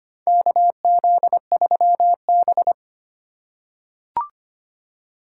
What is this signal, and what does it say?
Morse code, heard as a keyed tone, about 25 wpm, 700 Hz. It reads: KZ3B
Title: USA Call Signs - 4 Characters - 4x Character Spacing - 25wpm